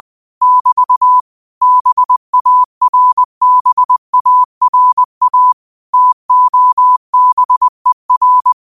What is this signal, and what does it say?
Morse code, heard as a keyed tone, about 20 wpm, 1000 Hz. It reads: <BT> BARBARA TOBER
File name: qotdfiles / 20wpm